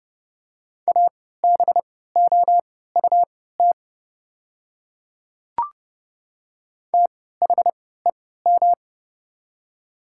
Morse code, heard as a keyed tone, about 30 wpm, spaced as for 17 wpm, 700 Hz. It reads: ABOUT THEM
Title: 3x Character Spacing - Top 100 Words - 30wpm